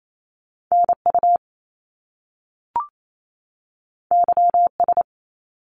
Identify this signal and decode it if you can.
Morse code, heard as a keyed tone, about 28 wpm, 700 Hz. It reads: NU YS
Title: Top Two Letter Combinations in Words 28wpm